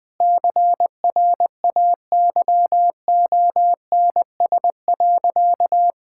Morse code, heard as a keyed tone, about 20 wpm, 700 Hz. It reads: CRAYONS.
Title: Easy Sentences with Hints - 20wpm